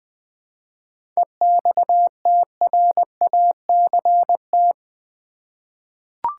Morse code, heard as a keyed tone, about 20 wpm, 690 Hz. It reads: EXTRACT
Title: 7 Letter Words - First 3 Letters Given - 20wpm